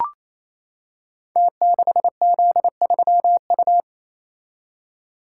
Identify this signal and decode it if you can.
Morse code, heard as a keyed tone, about 27 wpm, 700 Hz. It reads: T6Z3U
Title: Code Groups - 5 Letter:Numbers - 28wpm